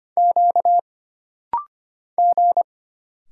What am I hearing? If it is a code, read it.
Morse code, heard as a keyed tone, about 25 wpm, 700 Hz. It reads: Q G
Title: Single Letters - Rapid-Fire 25wpm